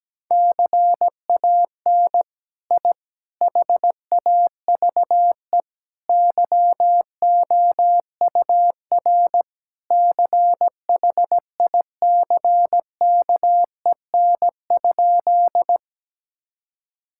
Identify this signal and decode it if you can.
Morse code, heard as a keyed tone, about 17 wpm, 700 Hz. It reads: CAN I HAVE YOUR CHICKEN?
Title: Sentences from Top 1000 Words - Review Part 5 - 17wpm